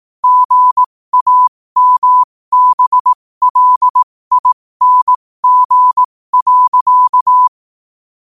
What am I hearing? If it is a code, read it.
Morse code, heard as a keyed tone, about 18 wpm, spaced as for 15 wpm, 1000 Hz. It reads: GAMBLING.
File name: qotdfiles / 15wpm